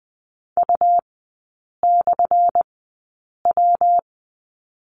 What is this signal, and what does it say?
Morse code, heard as a keyed tone, about 20 wpm, spaced as for 8 wpm, 700 Hz. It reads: U/W
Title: 184 Lesson 30e - Letter b - Three Characters 20wpm